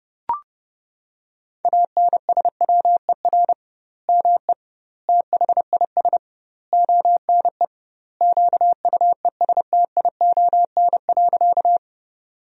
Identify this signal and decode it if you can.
Morse code, heard as a keyed tone, about 30 wpm, 700 Hz. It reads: ANSWER ME THIS ONE QUESTION.
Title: 2x Word-Spacing - Sentences from Top 400 Words - 30wpm